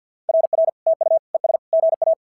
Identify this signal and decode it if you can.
Morse code, heard as a keyed tone, about 50 wpm, 650 Hz. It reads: RR TU ES GA
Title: SOTA QSO Practice 50wpm